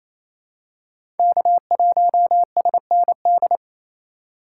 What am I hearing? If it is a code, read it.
Morse code, heard as a keyed tone, about 28 wpm, 700 Hz. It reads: K1SND
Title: K1USN SST QSO Practice - 28wpm